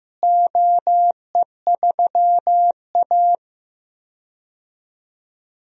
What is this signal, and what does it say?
Morse code, heard as a keyed tone, about 15 wpm, 700 Hz. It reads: OE3A
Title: DXCC-100 Call Signs - 4 Characters - Repeated 3x - 15wpm